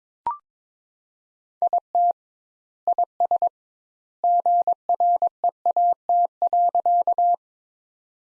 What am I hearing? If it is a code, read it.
Morse code, heard as a keyed tone, about 21 wpm, 700 Hz. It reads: IT IS GREAT.
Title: Sentences Top 200 Words - 22wpm